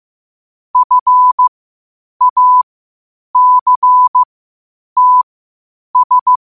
Morse code, heard as a keyed tone, about 15 wpm, spaced as for 8 wpm, 1000 Hz. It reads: FACTS
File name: qotdfiles / 8wpm